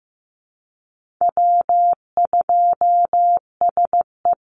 Morse code, heard as a keyed tone, about 15 wpm, 700 Hz. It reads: W2SE
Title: NA SOTA Chaser Callsigns 15wpm